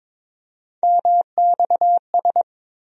Morse code, heard as a keyed tone, about 22 wpm, 700 Hz. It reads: MXS
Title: USA Call Sign Postfixes - 3 Letters 22wpm